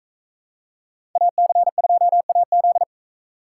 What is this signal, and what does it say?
Morse code, heard as a keyed tone, about 40 wpm, 700 Hz. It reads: AC2AZ